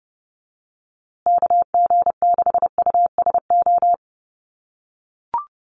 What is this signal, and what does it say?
Morse code, heard as a keyed tone, about 30 wpm, 700 Hz. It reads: KG6USO
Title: USA Call Signs Repeated 3x 30wpm